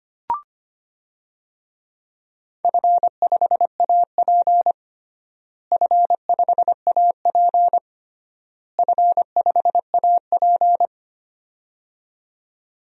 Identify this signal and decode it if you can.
Morse code, heard as a keyed tone, about 25 wpm, 700 Hz. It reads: F5AP F5AP F5AP
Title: DXCC-100 Call Signs - 4 Characters - Repeated 3x - 25wpm